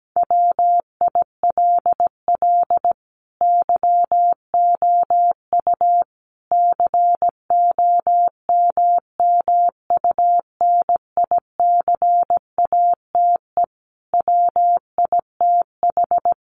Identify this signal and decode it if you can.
Morse code, heard as a keyed tone, about 17 wpm, 700 Hz. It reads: WILL YOU COMMUNICATE WITH
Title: Sentences from Top 1300 Words 17wpm